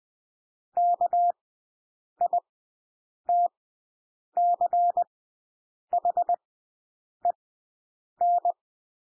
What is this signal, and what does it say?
Morse code, heard as a keyed tone, about 20 wpm, spaced as for 8 wpm, 700 Hz. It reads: KITCHEN
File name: cwmp3 / W1 SOUND WORDS 20 wpm